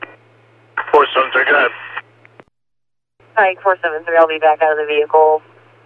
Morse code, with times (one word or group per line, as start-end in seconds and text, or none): none